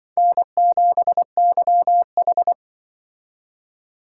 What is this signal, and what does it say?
Morse code, heard as a keyed tone, about 25 wpm, 700 Hz. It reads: N7YH
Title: Speed Racing - USA Call Signs 20wpm